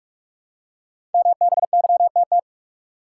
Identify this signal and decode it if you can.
Morse code, heard as a keyed tone, about 46 wpm, 700 Hz. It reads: MBYTT
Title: Code Groups - 5 Letter:Numbers - 45wpm